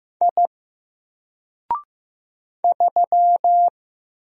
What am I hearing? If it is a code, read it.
Morse code, heard as a keyed tone, about 15 wpm, 700 Hz. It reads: I 3
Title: Single Character (Letter:Number:Punctuation:Prosign) 15wpm